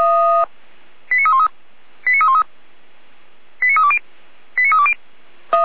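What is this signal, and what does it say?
Five-tone selective call (ZVEI), heard as quick runs of five tones, tones of about 70 ms. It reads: 89313 89313 89320 89320
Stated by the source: ZVEI five-tone selective call received by SDR, code 89315.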